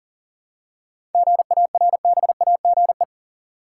Morse code, heard as a keyed tone, about 38 wpm, 700 Hz. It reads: GARBAGE
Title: Seven Letter Words - 40wpm